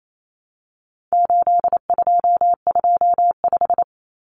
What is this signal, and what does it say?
Morse code, heard as a keyed tone, about 28 wpm, 700 Hz. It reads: 8225
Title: Numbers - 4-Digit 28wpm